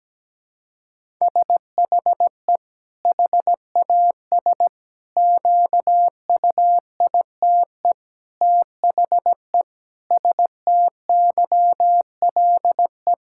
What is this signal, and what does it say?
Morse code, heard as a keyed tone, about 17 wpm, 700 Hz. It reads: SHE HAS QUITE THE STYLE
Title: Sentences from Top 1000 Words - Review Part 5 - 17wpm